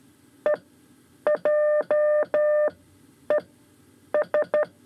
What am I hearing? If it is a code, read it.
Morse code, heard as a keyed tone, about 13 wpm, spaced as for 8 wpm, 600 Hz. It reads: EJES
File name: 5 Lezione CW 2